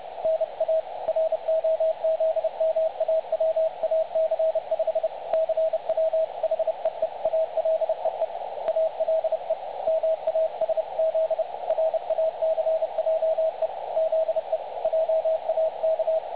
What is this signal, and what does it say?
Morse code, heard as a keyed tone, about 30 wpm, 660 Hz. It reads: NA ROZMAWAC 5 CW HEE ALE E ALE MASZ RACJE ZE JAK
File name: dlazaawansowanych / nagraniazpasma / websdr recording 3565.0kHz